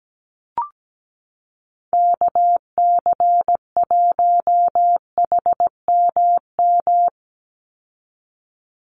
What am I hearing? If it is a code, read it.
Morse code, heard as a keyed tone, about 17 wpm, 700 Hz. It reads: KC1HMM
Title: USA Call Signs 17wpm